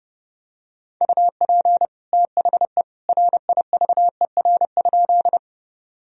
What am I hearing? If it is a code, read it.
Morse code, heard as a keyed tone, about 30 wpm, 700 Hz. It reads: UP THE RIVER?
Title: Sentences from Top 1000 Words - Review Part 1 - 30wpm